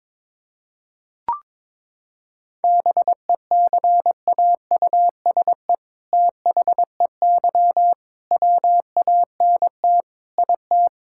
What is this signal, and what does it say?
Morse code, heard as a keyed tone, about 21 wpm, 700 Hz. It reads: BECAUSE THEY WANT IT
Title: Sentences from Top 1000 Words - Review Part 5 - 22wpm